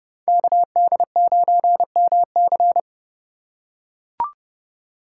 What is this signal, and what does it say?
Morse code, heard as a keyed tone, about 31 wpm, 700 Hz. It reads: KD9MC
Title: USA Call Signs - 5 Characters - 2x Character Spacing - 30wpm